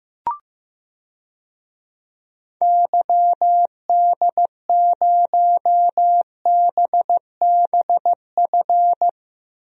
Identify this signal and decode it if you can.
Morse code, heard as a keyed tone, about 15 wpm, 690 Hz. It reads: YD0BBF
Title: DXCC-100 Call Signs - 6 Characters - 15wpm